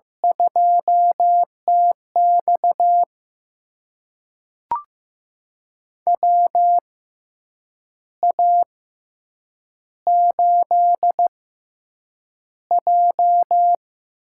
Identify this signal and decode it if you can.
Morse code, heard as a keyed tone, about 15 wpm, spaced as for 7 wpm, 700 Hz. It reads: 2TX WA8J
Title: USA Call Signs - 4 Characters - 6x Character Spacing - 15wpm